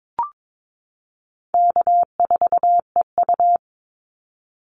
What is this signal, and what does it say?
Morse code, heard as a keyed tone, about 22 wpm, 700 Hz. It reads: K4EU
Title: Easy Call Signs - 22wpm